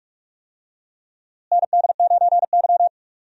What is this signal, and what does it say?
Morse code, heard as a keyed tone, about 46 wpm, 700 Hz. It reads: ND9Y
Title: USA Call Signs - 4 Characters - Repeated 3x - 45wpm